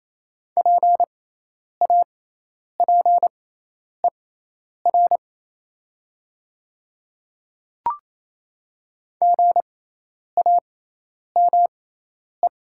Morse code, heard as a keyed tone, about 28 wpm, spaced as for 10 wpm, 700 Hz. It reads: PAPER GAME
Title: 6x Character Spacing - Top 500 Words - 28wpm